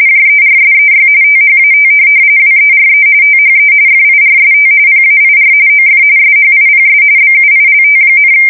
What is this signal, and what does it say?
RTTY, about 45.45 baud. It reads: JERSEY THAT WERE THE FIRST PAIR OF SHIPS BUILT,
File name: fsk170 / USS-Missouri